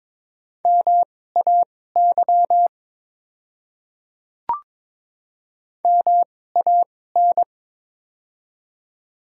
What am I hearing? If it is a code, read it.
Morse code, heard as a keyed tone, about 22 wpm, spaced as for 16 wpm, 700 Hz. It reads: MAY MAN
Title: 2x Character Spacing - Top 200 Words - 22wpm